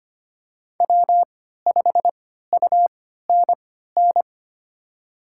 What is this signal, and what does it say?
Morse code, heard as a keyed tone, about 25 wpm, spaced as for 14 wpm, 700 Hz. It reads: W5UNN
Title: USA Call Signs - 5 Characters - 3x Character Spacing - 25wpm